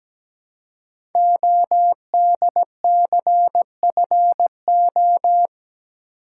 Morse code, heard as a keyed tone, about 17 wpm, 700 Hz. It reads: ODCFO
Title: Code Groups - 5 Letters - 17wpm